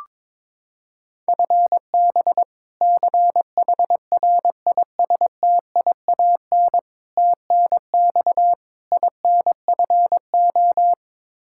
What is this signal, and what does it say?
Morse code, heard as a keyed tone, about 22 wpm, 700 Hz. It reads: FB CHRISTIAN TNX INFO
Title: K1USN SST QSO Practice 22wpm